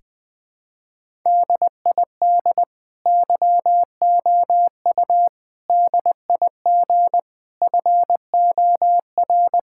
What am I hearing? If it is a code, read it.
Morse code, heard as a keyed tone, about 20 wpm, 700 Hz. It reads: DID YOU DIG FOR
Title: Sentences from Top 1600 Words 20wpm